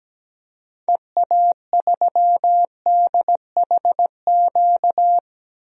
Morse code, heard as a keyed tone, about 17 wpm, 700 Hz. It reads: EA3DHQ
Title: DXCC-100 Call Signs - 6 Characters - 17wpm